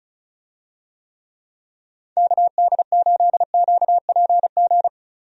Morse code, heard as a keyed tone, about 36 wpm, 700 Hz. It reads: KD8QPG
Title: USA Call Signs - 6 Characters - Repeated 3x - 35wpm